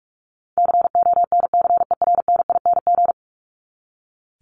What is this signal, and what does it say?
Morse code, heard as a keyed tone, about 44 wpm, 700 Hz. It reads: CONCERNING
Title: ING Suffix - Top 100 Words with Base Words 45wpm